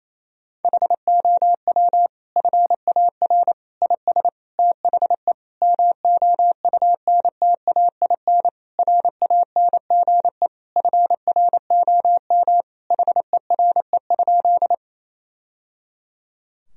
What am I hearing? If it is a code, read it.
Morse code, heard as a keyed tone, about 28 wpm, 700 Hz. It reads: HOW FAR IS THE MOUNTAIN RANGE FROM HERE?
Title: Sentences from Top 1000 Words - Review Part 4 - 28wpm